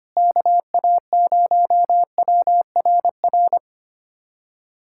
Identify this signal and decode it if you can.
Morse code, heard as a keyed tone, about 25 wpm, 700 Hz. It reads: KA0WRR
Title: USA Callsign DE Callsign 25wpm